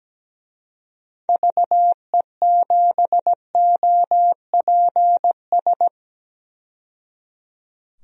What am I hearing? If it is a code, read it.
Morse code, heard as a keyed tone, about 17 wpm, 700 Hz. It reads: VE7OPS
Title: DXCC-100 Call Signs - 6 Characters - 17wpm